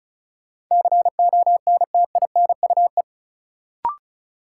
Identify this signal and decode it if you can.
Morse code, heard as a keyed tone, about 36 wpm, 700 Hz. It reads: CONTINUE
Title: Sentences from Top 1000 Words - Review Part 6 - New Word Drill - 35wpm